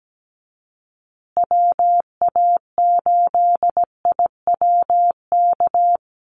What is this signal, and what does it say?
Morse code, heard as a keyed tone, about 17 wpm, 700 Hz. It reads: WA8IWK
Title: Call Signs Repeated 2x - 17wpm